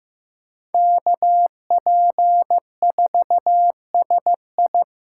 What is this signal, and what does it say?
Morse code, heard as a keyed tone, about 15 wpm, 700 Hz. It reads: KP4SI
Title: USA Call Signs - 5 Characters - 8x Character Spacing - 15wpm